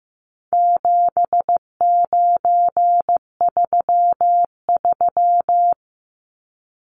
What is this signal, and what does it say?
Morse code, heard as a keyed tone, about 15 wpm, 700 Hz. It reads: 7933
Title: Numbers - 4-Digit 15wpm